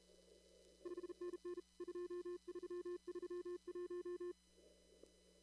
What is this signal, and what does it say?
Morse code, heard as a keyed tone, about 33 wpm, 350 Hz. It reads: 5NN 2331
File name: CW / kt5j wpx cw-op-k5pi